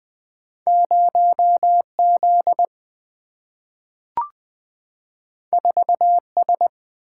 Morse code, heard as a keyed tone, about 20 wpm, 700 Hz. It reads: 0 Z 4 S
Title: Single-Letter Single-Number 20wpm